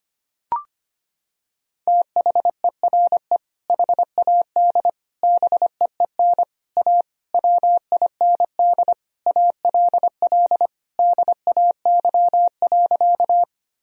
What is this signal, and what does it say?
Morse code, heard as a keyed tone, about 25 wpm, 700 Hz. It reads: THERE HAD BEEN A WIND ALL DAY.
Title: Sentences from Top 1000 Words - Review Part 2 - 25wpm